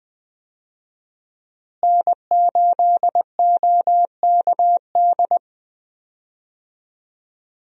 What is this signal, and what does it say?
Morse code, heard as a keyed tone, about 20 wpm, 700 Hz. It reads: N8OKD
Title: USA Call Signs - 5 Characters - Repeated 2x - 20wpm